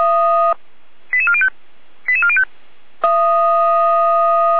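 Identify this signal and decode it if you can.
Five-tone selective call (ZVEI), heard as quick runs of five tones, tones of about 70 ms. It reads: 88405 88405
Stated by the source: Five-tone ZVEI selective call received on SDR.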